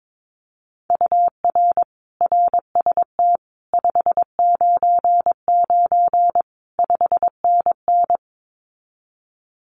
Morse code, heard as a keyed tone, about 22 wpm, 700 Hz. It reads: UR RST 599 5NN
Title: Common QSO Phrases Encore 22wpm